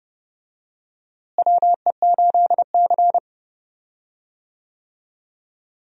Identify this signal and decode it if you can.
Morse code, heard as a keyed tone, about 30 wpm, 700 Hz. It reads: WE8C
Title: USA Call Signs - 4 Characters - Repeated 2x - 30wpm